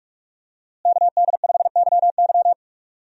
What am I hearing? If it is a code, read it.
Morse code, heard as a keyed tone, about 43 wpm, 700 Hz. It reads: KD5YY
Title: USA Call Signs - 5 Characters - Repeated 2x - 45wpm